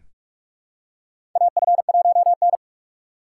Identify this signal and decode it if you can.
Morse code, heard as a keyed tone, about 43 wpm, 700 Hz. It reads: AF1N
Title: USA Call Signs - 4 Characters - 6x Character Spacing - 45wpm